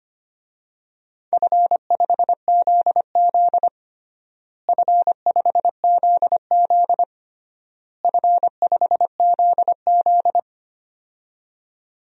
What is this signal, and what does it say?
Morse code, heard as a keyed tone, about 25 wpm, 700 Hz. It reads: F5ZZ F5ZZ F5ZZ
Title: DXCC-100 Call Signs - 4 Characters - Repeated 3x - 25wpm